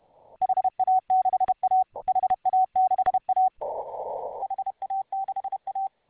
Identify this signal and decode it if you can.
Morse code, heard as a keyed tone, about 31 wpm, 750 Hz. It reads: HA6A HA6A HA6A